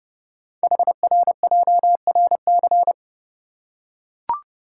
Morse code, heard as a keyed tone, about 29 wpm, 700 Hz. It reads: HRJRC
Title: Code Groups - 5 Letters - 30wpm